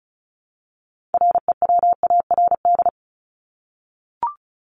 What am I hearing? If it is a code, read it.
Morse code, heard as a keyed tone, about 34 wpm, 700 Hz. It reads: REWARD
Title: Top 1751-2000 Words 35wpm